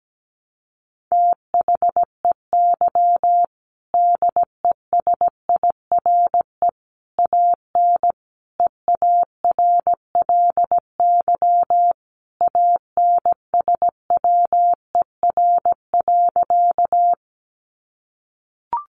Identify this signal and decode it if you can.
Morse code, heard as a keyed tone, about 17 wpm, 700 Hz. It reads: THEY DESIRE AN EARLY ANSWER.
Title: Sentences from Top 1100 Words 17wpm